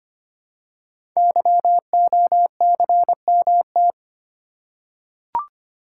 Morse code, heard as a keyed tone, about 25 wpm, 700 Hz. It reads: YOCMT
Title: Code Groups - 5 Letters - 25wpm